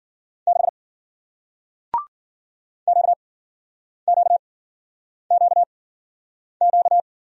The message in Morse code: 6 QQQQ

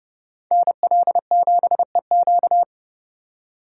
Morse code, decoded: NL7EQ

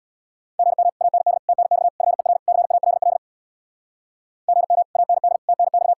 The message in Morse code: GM AND TNX FER CALL GM AND TNX

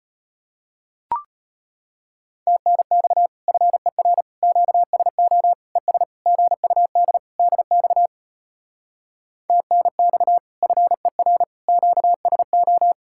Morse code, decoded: TNX FER QSO ES GUD DX TNX FER QSO

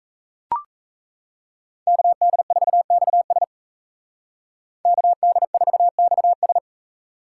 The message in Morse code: KD4XS KD4XS

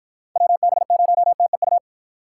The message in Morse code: WB0TEU

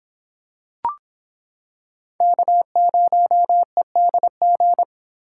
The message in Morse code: K0EDG

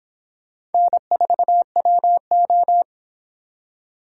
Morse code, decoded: N4WO

